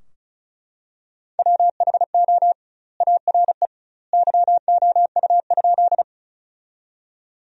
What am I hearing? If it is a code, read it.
Morse code, heard as a keyed tone, about 34 wpm, 700 Hz. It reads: WHO ARE YOU?
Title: 2x Word-Spacing - Sentences from Top 100 Words - 35wpm